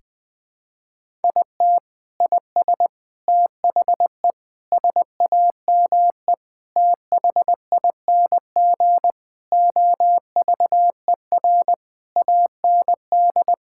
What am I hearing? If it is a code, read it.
Morse code, heard as a keyed tone, about 20 wpm, 700 Hz. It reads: IT IS THE SAME THING OVER AND
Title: Sentences with 6-Word Phrases - 20wpm